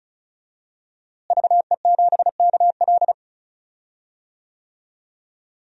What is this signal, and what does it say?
Morse code, heard as a keyed tone, about 34 wpm, 700 Hz. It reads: VE7KL
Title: DXCC-100 Call Signs - 5 Characters - Repeated 3x - 35wpm